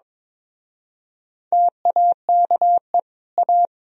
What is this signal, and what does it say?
Morse code, heard as a keyed tone, about 25 wpm, 700 Hz. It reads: TAKE A
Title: Speed-Racing - Sets of 4 Words 22wpm